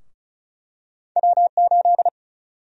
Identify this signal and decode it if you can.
Morse code, heard as a keyed tone, about 34 wpm, 700 Hz. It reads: W8
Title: USA Call Sign Prefixes - Repeated 2x - 35wpm